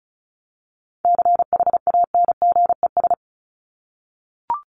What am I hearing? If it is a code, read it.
Morse code, heard as a keyed tone, about 34 wpm, 700 Hz. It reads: CHANGES
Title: ES:S Suffix - Top 100 Words with Base Words 35wpm